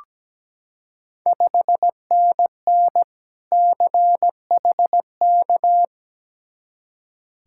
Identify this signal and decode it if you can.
Morse code, heard as a keyed tone, about 17 wpm, 700 Hz. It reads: 5NN CHK
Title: Ontario QSO Party Exchange 17wpm